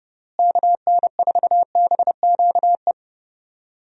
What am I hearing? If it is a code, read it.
Morse code, heard as a keyed tone, about 30 wpm, 700 Hz. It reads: KN4BQE
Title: USA Call Signs - 6 Characters - Repeated 2x - 30wpm